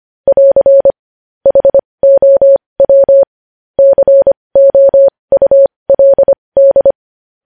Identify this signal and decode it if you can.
Morse code, heard as a keyed tone, about 25 wpm, spaced as for 20 wpm, 550 Hz. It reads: <AR> HOW COULD
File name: podcast / News headlines-1x-25x20